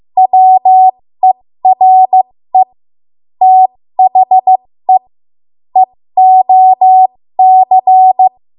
Morse code, heard as a keyed tone, about 15 wpm, spaced as for 13 wpm, 750 Hz. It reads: WERE THE EOC
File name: Morse / 13WPM